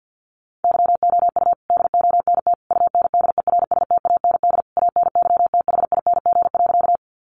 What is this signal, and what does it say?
Morse code, heard as a keyed tone, about 48 wpm, 700 Hz. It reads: YOU DONT UNDERSTAND ANYTHING.
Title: Sentences from Top 1000 Words 50wpm